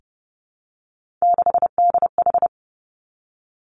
Morse code, heard as a keyed tone, about 30 wpm, 700 Hz. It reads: 6DH
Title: Three Challenging Letter-Number Combinations 30wpm